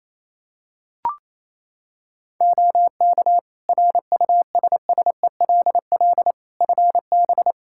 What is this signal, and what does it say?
Morse code, heard as a keyed tone, about 28 wpm, 700 Hz. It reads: OK RUSSELL FB
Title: Practice QSO - Exchange 2 - Info - 28wpm